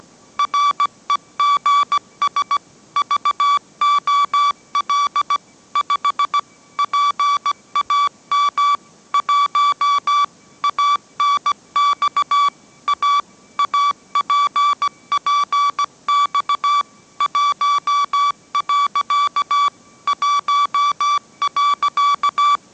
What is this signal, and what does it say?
Morse code, heard as a keyed tone, about 17 wpm, 1150 Hz. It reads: REGS VOL 5 PAM 1 ANX A APPX 1. 1.
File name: mrx / mrx080 18wpm